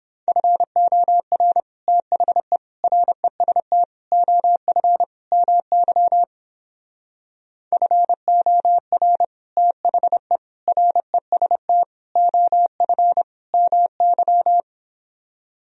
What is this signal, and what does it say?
Morse code, heard as a keyed tone, about 29 wpm, 700 Hz. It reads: FOR THE REST OF MY FOR THE REST OF MY